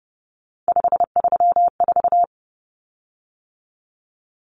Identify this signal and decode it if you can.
Morse code, heard as a keyed tone, about 28 wpm, 700 Hz. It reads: 534
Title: Common Cut Numbers - 3-Digit 30wpm